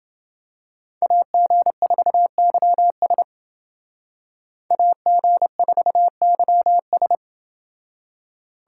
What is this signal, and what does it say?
Morse code, heard as a keyed tone, about 29 wpm, 700 Hz. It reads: AG4YS AG4YS